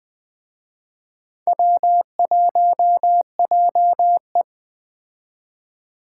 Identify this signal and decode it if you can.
Morse code, heard as a keyed tone, about 20 wpm, 700 Hz. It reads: W1JE